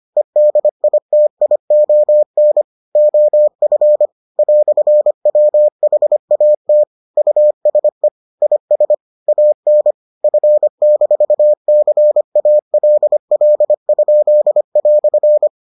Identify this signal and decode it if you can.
Morse code, heard as a keyed tone, about 25 wpm, 600 Hz. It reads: EDITION OF "WHAT USE IS AN F-CALL?"